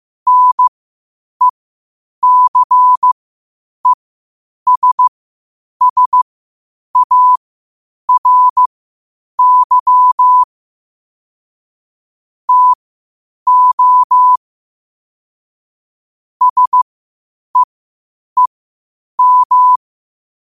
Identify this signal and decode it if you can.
Morse code, heard as a keyed tone, about 15 wpm, spaced as for 8 wpm, 1000 Hz. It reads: NECESSARY TO SEEM
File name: qotdfiles / 8wpm